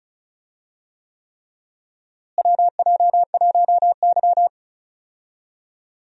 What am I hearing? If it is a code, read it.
Morse code, heard as a keyed tone, about 36 wpm, 700 Hz. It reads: WJ1Y
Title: USA Call Signs - 4 Characters - Repeated 3x - 35wpm